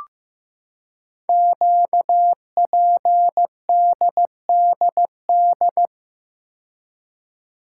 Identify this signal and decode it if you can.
Morse code, heard as a keyed tone, about 15 wpm, 700 Hz. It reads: QPDDD
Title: Code Groups - 5 Letters - 15wpm